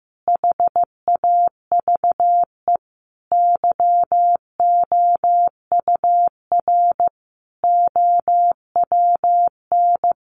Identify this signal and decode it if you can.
Morse code, heard as a keyed tone, about 15 wpm, 700 Hz. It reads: HAVE YOUR OWN
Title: Sentences from Top 1200 Words 15wpm